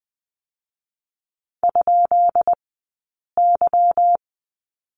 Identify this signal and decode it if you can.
Morse code, heard as a keyed tone, about 20 wpm, 700 Hz. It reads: ? Y
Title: 177 Lesson 29d - Slash - Two Characters 20wpm